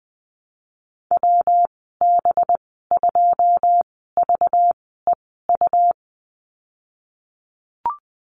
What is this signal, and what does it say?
Morse code, heard as a keyed tone, about 19 wpm, spaced as for 14 wpm, 700 Hz. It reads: WB2VEU
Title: 193 Lesson 31g - Letter v - Call Signs 20wpm